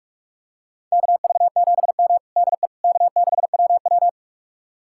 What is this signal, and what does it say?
Morse code, heard as a keyed tone, about 47 wpm, 700 Hz. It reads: KV7M DE K6WW
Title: USA Callsign DE Callsign - 4 Characters - 45wpm